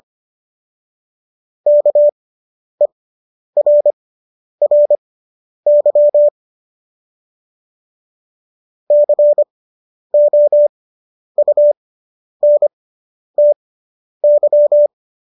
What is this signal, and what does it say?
Morse code, heard as a keyed tone, about 25 wpm, spaced as for 10 wpm, 590 Hz. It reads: KERRY COUNTY